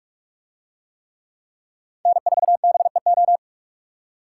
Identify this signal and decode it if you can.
Morse code, heard as a keyed tone, about 42 wpm, 700 Hz. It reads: N4BEO